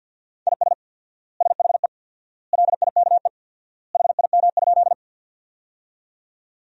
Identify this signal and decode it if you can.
Morse code, heard as a keyed tone, about 50 wpm, 700 Hz. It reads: IS SHE LIKE HIM?
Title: 4x Word-Spacing - Sentences from Top 100 Words - 50wpm